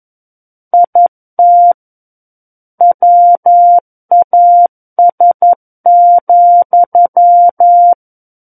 Morse code, heard as a keyed tone, about 11 wpm, 700 Hz. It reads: IT WAS,
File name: Book One / 02-Chapter 2 - 11 WPM